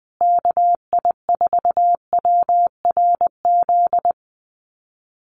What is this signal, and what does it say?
Morse code, heard as a keyed tone, about 20 wpm, 700 Hz. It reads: KI4WRZ
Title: USA Call Signs Repeated 3x 20wpm